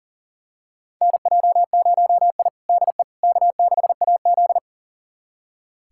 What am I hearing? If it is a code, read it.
Morse code, heard as a keyed tone, about 40 wpm, 700 Hz. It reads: NJ0I DE K6AZ
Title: USA Callsign DE Callsign - 4 Characters - 40wpm